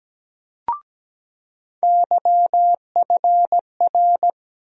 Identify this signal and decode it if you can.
Morse code, heard as a keyed tone, about 17 wpm, 700 Hz. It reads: YFR
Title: USA Call Sign Postfixes - 3 Letters 17wpm